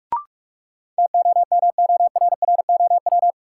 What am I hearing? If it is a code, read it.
Morse code, heard as a keyed tone, about 43 wpm, 700 Hz. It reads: TOMORROW
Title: Sentences from Top 1100 Words - New Word Drill Alt - 45wpm